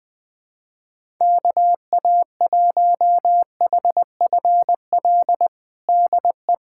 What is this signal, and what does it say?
Morse code, heard as a keyed tone, about 20 wpm, 700 Hz. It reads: KA1HFL DE
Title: USA Callsign DE Callsign 20wpm